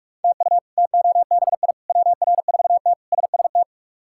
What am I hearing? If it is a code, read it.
Morse code, heard as a keyed tone, about 43 wpm, 700 Hz. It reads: TU TOBI WR4T SST
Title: K1USN SST QSO Practice - 45wpm